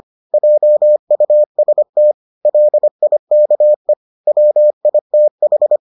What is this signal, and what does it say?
Morse code, heard as a keyed tone, about 25 wpm, 600 Hz. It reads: JUST LIKE WITH